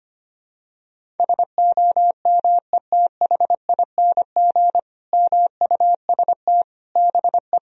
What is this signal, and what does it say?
Morse code, heard as a keyed tone, about 25 wpm, 700 Hz. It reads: SOMETHING MUST BE
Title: Sentences from Top 1000 Words - Review Part 1 - 25wpm